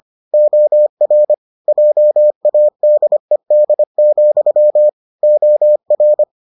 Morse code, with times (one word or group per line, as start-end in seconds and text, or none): 0.3-1.3 OR
1.7-4.9 JADED,
5.2-6.2 OR